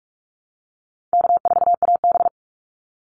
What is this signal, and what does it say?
Morse code, heard as a keyed tone, about 44 wpm, 700 Hz. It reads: K4AB
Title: Call Signs Repeated 2x - 45wpm